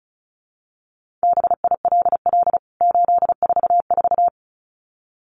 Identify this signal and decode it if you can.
Morse code, heard as a keyed tone, about 34 wpm, 700 Hz. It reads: BILL 844
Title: CWT Exchange 35wpm